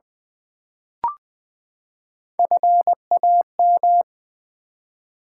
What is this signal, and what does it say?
Morse code, heard as a keyed tone, about 20 wpm, 700 Hz. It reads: FAM
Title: Build A Word - 3 Letters At A Time - 8-Letter Words - 20wpm